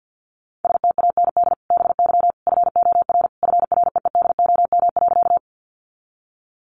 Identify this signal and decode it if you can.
Morse code, heard as a keyed tone, about 50 wpm, 700 Hz. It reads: STAND BY FOR FREEDOM.
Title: Sentences from Top 800 Words 50wpm